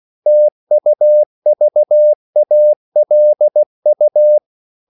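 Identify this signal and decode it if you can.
Morse code, heard as a keyed tone, about 16 wpm, 600 Hz. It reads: TUVALU
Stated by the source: CW practice of national capitals at 16 words per minute, from ON6ZQ.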